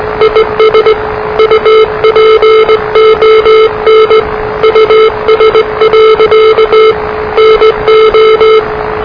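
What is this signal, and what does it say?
Morse code, heard as a keyed tone, about 19 wpm, 440 Hz, with noise. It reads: IS UPON US. NO
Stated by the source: Morse challenge from NZ Net, the national amateur radio net of Aotearoa/New Zealand, which meets on 3535.0 kHz.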